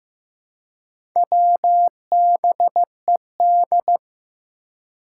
Morse code, heard as a keyed tone, about 15 wpm, 700 Hz. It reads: WBED
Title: Four Challenging Letter-Number Combinations 15wpm